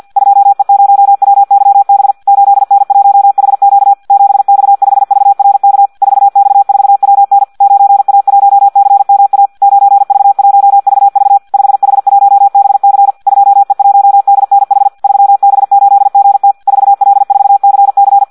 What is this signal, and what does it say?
Morse code, heard as a keyed tone, about 50 wpm, 800 Hz. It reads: 2E0WYB 8N1HQ 7X5VRK 4X4WN 9A1CMA 9V1VV 5H1BC 2E1DNH 3B8GT 4L4CC
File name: Audio practica CW / qrq-tx-50 2